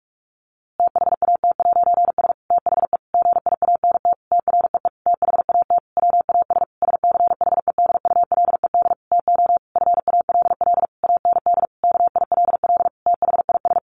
Morse code, heard as a keyed tone, about 44 wpm, 700 Hz. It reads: THAT'S THE GIANT TREE THAT WAS SCHEDULED TO FALL AND KILL THIS
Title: A Sound Of Thunder 45wpm